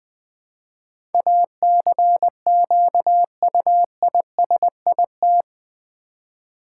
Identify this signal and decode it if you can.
Morse code, heard as a keyed tone, about 20 wpm, 700 Hz. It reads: ACQUISIT
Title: Build A Word - 4 Letters At A Time - 11-Letter Words - 20wpm